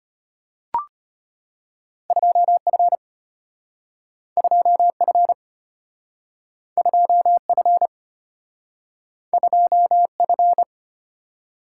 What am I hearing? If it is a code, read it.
Morse code, heard as a keyed tone, about 34 wpm, 700 Hz. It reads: 2F 2F 2F 2F